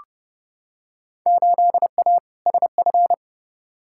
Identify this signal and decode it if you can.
Morse code, heard as a keyed tone, about 29 wpm, 700 Hz. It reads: 8A SF
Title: Field Day Exchanges Encore - 30wpm